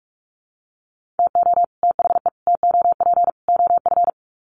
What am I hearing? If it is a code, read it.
Morse code, heard as a keyed tone, about 44 wpm, 700 Hz. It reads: TO THE TOP OF
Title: Sets of 4 Words 45wpm